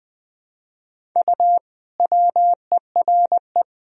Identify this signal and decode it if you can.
Morse code, heard as a keyed tone, about 21 wpm, 700 Hz. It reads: U WERE
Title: Sentences from Top 400 Words and Abbreviations 20wpm